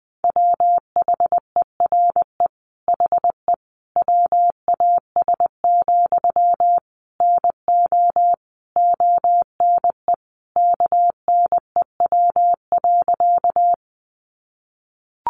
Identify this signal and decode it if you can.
Morse code, heard as a keyed tone, about 20 wpm, 700 Hz. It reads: WHERE HE WAS, NO ONE KNEW.
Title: Sentences from Top 400 Words 20wpm